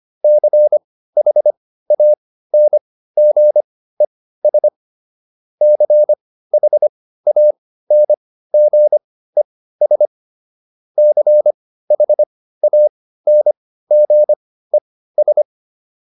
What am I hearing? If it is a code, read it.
Morse code, heard as a keyed tone, about 25 wpm, spaced as for 15 wpm, 600 Hz. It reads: CHANGES CHANGES CHANGES